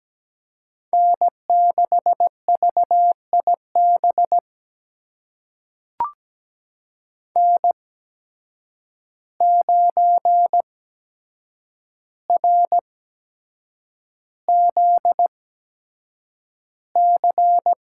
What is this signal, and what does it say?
Morse code, heard as a keyed tone, about 17 wpm, spaced as for 7 wpm, 700 Hz. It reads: N6VIB N9RZC
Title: USA Call Signs - 8x Character Spacing - 17wpm